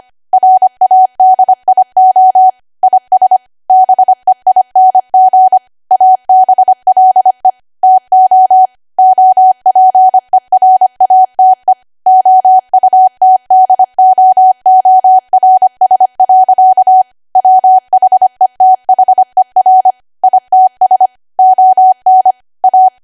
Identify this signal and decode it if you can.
Morse code, heard as a keyed tone, about 25 wpm, 750 Hz. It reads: RADIO IS BEING ABLE TO OPERATE OUTDOORS. WHETHER ITS ON A
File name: Archive / 25 WPM / 25WPM